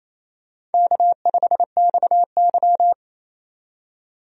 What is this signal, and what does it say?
Morse code, heard as a keyed tone, about 28 wpm, 700 Hz. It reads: K5XY
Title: USA Call Signs - 4 Characters - 28wpm